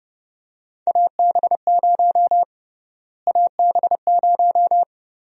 Morse code, heard as a keyed tone, about 29 wpm, 700 Hz. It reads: AB0 AB0
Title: USA Call Sign Prefixes - Repeated 2x - 30wpm